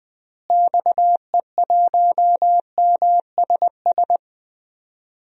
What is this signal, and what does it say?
Morse code, heard as a keyed tone, about 20 wpm, 700 Hz. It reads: XE1MSS
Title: DXCC-100 Call Signs - 6 Characters - Repeated 3x - 20wpm